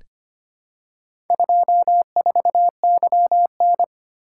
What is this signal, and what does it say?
Morse code, heard as a keyed tone, about 25 wpm, 700 Hz. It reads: 24YN
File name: practice / 25wpm / Four Letter-Number Combinations - 25wpm